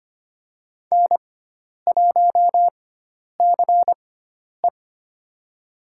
Morse code, heard as a keyed tone, about 25 wpm, spaced as for 10 wpm, 700 Hz. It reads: N1CE
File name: Lesson 16g - Letter c - Call Signs 25wpm